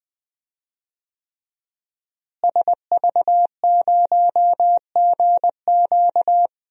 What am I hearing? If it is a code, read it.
Morse code, heard as a keyed tone, about 21 wpm, 700 Hz. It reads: SV0GQ
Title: DXCC-100 Call Signs - 5 Characters - Repeated 3x - 20wpm